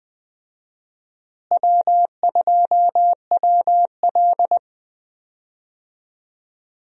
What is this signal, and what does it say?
Morse code, heard as a keyed tone, about 20 wpm, 700 Hz. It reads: W2WL
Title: USA Call Signs - 4 Characters - Repeated 3x - 20wpm